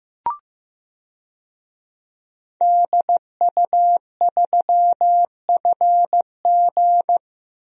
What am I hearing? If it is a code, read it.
Morse code, heard as a keyed tone, about 15 wpm, 700 Hz. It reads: DU3FG
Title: DXCC-100 Call Signs - 5 Characters - Repeated 2x - 15wpm